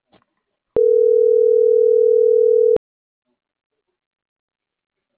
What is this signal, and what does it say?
Ringback tone (North America), 440 + 480 Hz.